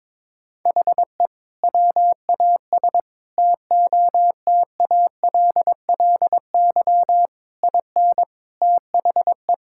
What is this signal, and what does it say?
Morse code, heard as a keyed tone, about 22 wpm, 700 Hz. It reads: HE WAS TOTALLY IN THE
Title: Sentences from Top 1600 Words 22wpm